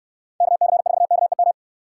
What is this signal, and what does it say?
Morse code, heard as a keyed tone, about 65 wpm, 700 Hz. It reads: KC4KEM